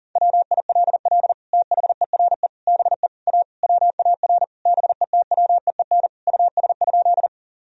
Morse code, heard as a keyed tone, about 40 wpm, 700 Hz. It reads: WILL THERE BE A WAR BETWEEN US?
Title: Sentences from Top 1000 Words - Review Part 2 - 40wpm